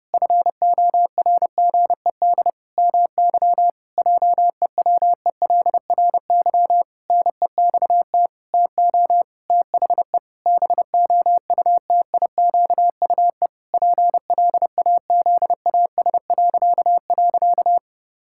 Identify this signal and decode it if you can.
Morse code, heard as a keyed tone, about 30 wpm, 700 Hz. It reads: FORGED MY JEWELRY NEXT TO THE BOUTIQUE PLAZAS..
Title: Pangram Challenge v2 - 30wpm